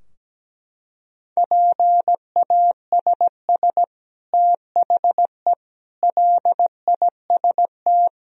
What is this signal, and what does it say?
Morse code, heard as a keyed tone, about 17 wpm, 700 Hz. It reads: PASS THE LIST
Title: Sentences from Top 1000 Words - Review Part 2 - 17wpm